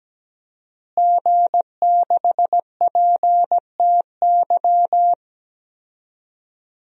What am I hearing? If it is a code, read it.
Morse code, heard as a keyed tone, about 17 wpm, 700 Hz. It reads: G6PTY
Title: DXCC-100 Call Signs - 5 Characters - Repeated 2x - 17wpm